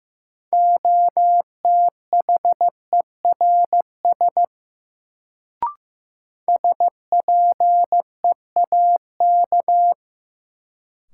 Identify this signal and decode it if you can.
Morse code, heard as a keyed tone, about 15 wpm, 700 Hz. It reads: OTHERS SPEAK
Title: Sentences from Top 1000 Words - Review Part 2 - New Word Drill - 15wpm